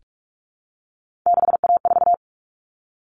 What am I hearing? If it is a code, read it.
Morse code, heard as a keyed tone, about 44 wpm, 700 Hz. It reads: 6A4
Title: Common Cut Numbers - 3-Digit 45wpm